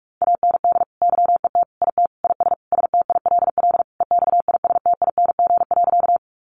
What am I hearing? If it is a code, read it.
Morse code, heard as a keyed tone, about 44 wpm, 700 Hz. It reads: AND YET IT IS STILL EXISTING.
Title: Sentences from Top 1500 Words 45wpm